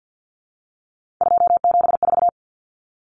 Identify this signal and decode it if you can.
Morse code, heard as a keyed tone, about 50 wpm, 700 Hz. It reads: 274
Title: Numbers - 3-Digit 50wpm